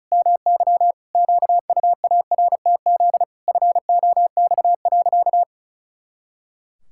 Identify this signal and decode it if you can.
Morse code, heard as a keyed tone, about 36 wpm, 700 Hz. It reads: MY QUARTZ FOX.
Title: Pangram Challenge v2 - 35wpm